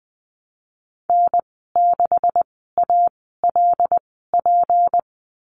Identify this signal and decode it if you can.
Morse code, heard as a keyed tone, about 20 wpm, spaced as for 15 wpm, 700 Hz. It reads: N6ALP
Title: 154 Lesson 25g - Letter p - Call Signs 20wpm